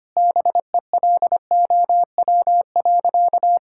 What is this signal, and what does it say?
Morse code, heard as a keyed tone, about 25 wpm, 700 Hz. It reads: BELOW.
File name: practice / 25wpm / Sentences with 9-Word Phrases - 25wpm